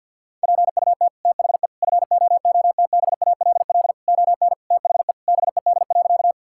Morse code, heard as a keyed tone, about 50 wpm, 700 Hz. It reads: PUT THE FOOTBALL ON THE BED.
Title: Sentences from Top 1600 Words 50wpm